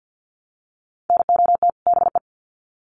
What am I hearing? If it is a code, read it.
Morse code, heard as a keyed tone, about 50 wpm, 700 Hz. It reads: NOT BE
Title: Sets of 2 Words 50wpm